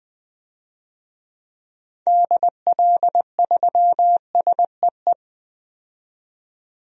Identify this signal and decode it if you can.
Morse code, heard as a keyed tone, about 20 wpm, 700 Hz. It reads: DL3SEE
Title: DXCC-100 Call Signs - 6 Characters - 20wpm